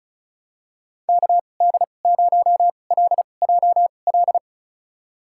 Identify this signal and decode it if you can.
Morse code, heard as a keyed tone, about 34 wpm, spaced as for 25 wpm, 700 Hz. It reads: KD0LJL
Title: USA Call Signs - 2x Character Spacing - 35wpm